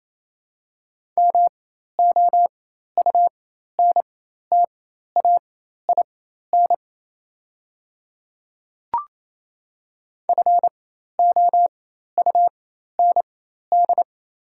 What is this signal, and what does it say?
Morse code, heard as a keyed tone, about 28 wpm, spaced as for 13 wpm, 700 Hz. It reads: MOUNTAIN FOUND
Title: 4x Character Spacing - Top 500 Words - 28wpm